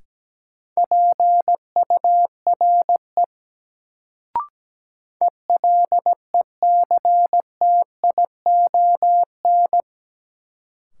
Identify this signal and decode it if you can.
Morse code, heard as a keyed tone, about 17 wpm, 700 Hz. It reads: PURE ELECTION
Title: Sentences from Top 1500 Words - New Word Drill Alt - 17wpm